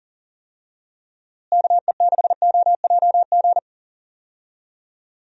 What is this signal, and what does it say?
Morse code, heard as a keyed tone, about 40 wpm, 700 Hz. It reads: KE6OJG
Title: USA Call Signs - 6 Characters - Repeated 3x - 40wpm